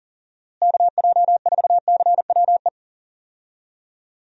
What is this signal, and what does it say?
Morse code, heard as a keyed tone, about 40 wpm, 700 Hz. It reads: KJ4CWE
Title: USA Call Signs - 6 Characters - Repeated 2x - 40wpm